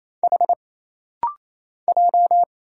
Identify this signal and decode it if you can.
Morse code, heard as a keyed tone, about 27 wpm, 700 Hz. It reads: H J
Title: Single Letters - Rapid-Fire 28wpm